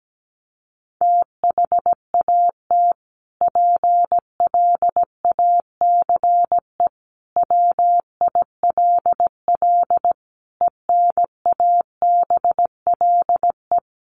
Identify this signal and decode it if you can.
Morse code, heard as a keyed tone, about 17 wpm, 700 Hz. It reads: THAT PLACE WILL ENABLE
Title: Sentences from Top 1500 Words 17wpm